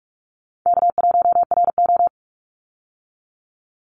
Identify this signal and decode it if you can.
Morse code, heard as a keyed tone, about 44 wpm, 700 Hz. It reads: K1RO
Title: Easy Call Signs Repeated 2x - 45wpm